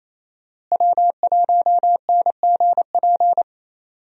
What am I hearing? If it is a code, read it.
Morse code, heard as a keyed tone, about 27 wpm, 700 Hz. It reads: W1NGP
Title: USA Call Signs - 5 Characters - Repeated 3x - 28wpm